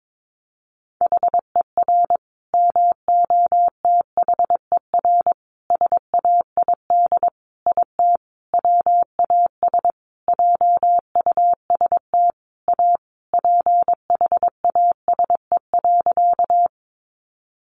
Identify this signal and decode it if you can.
Morse code, heard as a keyed tone, about 22 wpm, 700 Hz. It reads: HER MOTHER SAID IT WAS JUST A PHASE.
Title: Sentences from Top 1200 Words 22wpm